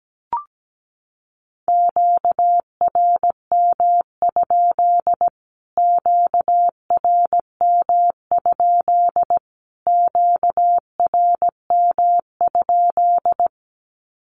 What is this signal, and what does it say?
Morse code, heard as a keyed tone, about 17 wpm, 700 Hz. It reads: QRM? QRM? QRM?
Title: Q-Codes Repeated 3x 17wpm